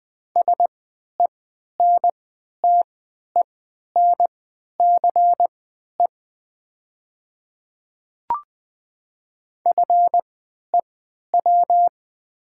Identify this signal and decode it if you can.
Morse code, heard as a keyed tone, about 20 wpm, spaced as for 11 wpm, 690 Hz. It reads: SENTENCE FEW
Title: 3x Character Spacing - Top 500 Words - 20wpm